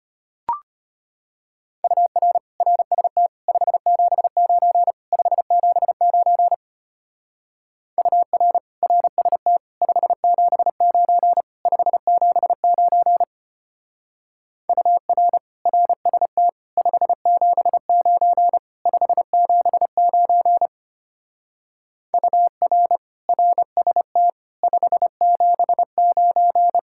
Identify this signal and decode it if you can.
Morse code, heard as a keyed tone, about 30 wpm, 700 Hz. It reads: UR RST 579 579 UR RST 579 579 UR RST 579 579 UR RST 579
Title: Speed-Racing Common QSO Phrases - 25wpm